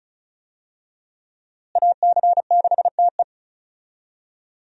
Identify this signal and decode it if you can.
Morse code, heard as a keyed tone, about 36 wpm, 700 Hz. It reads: AC6TE
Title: USA Call Signs - 5 Characters - Repeated 2x - 35wpm